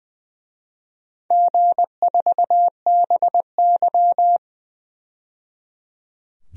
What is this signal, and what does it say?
Morse code, heard as a keyed tone, about 20 wpm, 700 Hz. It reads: G4BY
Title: DXCC-100 Call Signs - 4 Characters - 20wpm